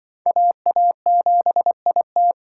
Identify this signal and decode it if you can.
Morse code, heard as a keyed tone, about 24 wpm, 700 Hz. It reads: AA7IT
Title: Speed Racing - USA 2x2 Call Signs 20wpm